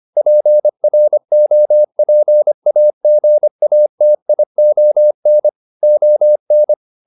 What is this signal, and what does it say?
Morse code, heard as a keyed tone, about 25 wpm, 600 Hz. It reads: PROPAGATION ON
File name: SS311 25